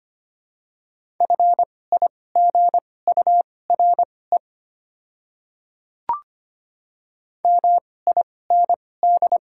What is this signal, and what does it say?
Morse code, heard as a keyed tone, about 25 wpm, spaced as for 18 wpm, 700 Hz. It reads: FIGURE MIND
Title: 2x Character Spacing - Top 500 Words - 25wpm